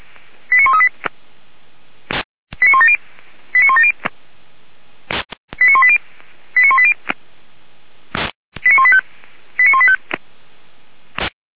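Five-tone selective call (ZVEI), heard as quick runs of five tones, tones of about 70 ms. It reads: 89138 89170 89170 89180 89180 89185 89185
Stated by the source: ZVEI five-tone call received by SDR, logged by fire brigade Feuerwehr Orlamünde.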